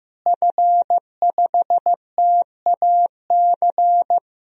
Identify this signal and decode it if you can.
Morse code, heard as a keyed tone, about 16 wpm, 700 Hz. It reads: F5TAC
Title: DXCC-100 Call Signs - 5 Characters - Repeated 2x - 15wpm